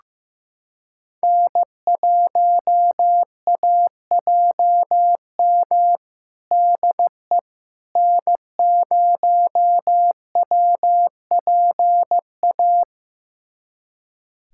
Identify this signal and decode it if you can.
Morse code, heard as a keyed tone, about 15 wpm, 700 Hz. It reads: N1AJM DE N0WPA
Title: USA Callsign DE Callsign 15wpm